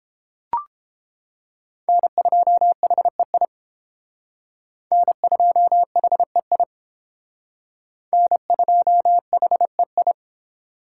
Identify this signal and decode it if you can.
Morse code, heard as a keyed tone, about 31 wpm, 700 Hz. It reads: N2HEI N2HEI N2HEI